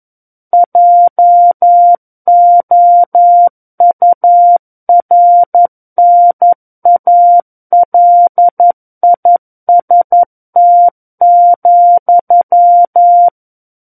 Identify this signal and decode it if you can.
Morse code, heard as a keyed tone, about 11 wpm, 700 Hz. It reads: JOURNALIST,
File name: Book One / 02-Chapter 2 - 11 WPM